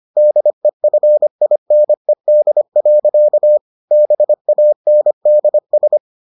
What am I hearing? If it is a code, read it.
Morse code, heard as a keyed tone, about 25 wpm, 600 Hz. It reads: DEFINED. BANDS